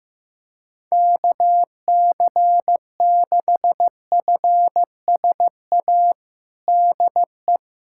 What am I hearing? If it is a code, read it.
Morse code, heard as a keyed tone, about 15 wpm, 700 Hz. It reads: KC6FSA DE
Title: USA Callsign DE Callsign 15wpm